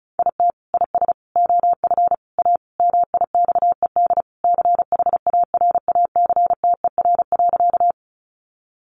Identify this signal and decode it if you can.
Morse code, heard as a keyed tone, about 34 wpm, 700 Hz. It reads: IT IS OF A MIXED CHARACTER.
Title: Sentences from Top 1500 Words 35wpm